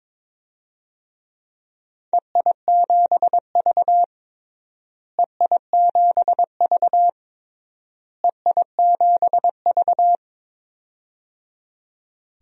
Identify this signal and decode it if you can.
Morse code, heard as a keyed tone, about 22 wpm, 700 Hz. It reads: EI7V EI7V EI7V
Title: DXCC-100 Call Signs - 4 Characters - Repeated 3x - 22wpm